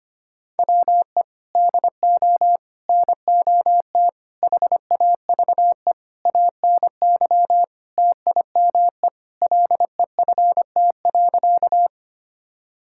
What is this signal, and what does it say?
Morse code, heard as a keyed tone, about 25 wpm, 700 Hz. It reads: WE DO NOT HAVE ANY TIME LEFT.
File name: practice / 25wpm / Sentences with 4-Word Phrases - 25wpm